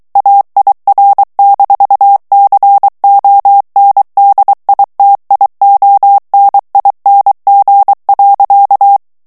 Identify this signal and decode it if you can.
Morse code, heard as a keyed tone, about 23 wpm, 800 Hz. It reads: AIR-CONDITIONING.
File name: morse mp3 / ANGLAIS texte clair 2 vitesse 22 mots minute